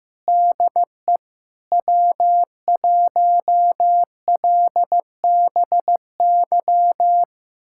Morse code, heard as a keyed tone, about 15 wpm, 700 Hz. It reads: DE W1LBY
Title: USA Callsign DE Callsign - 5 Characters - 15wpm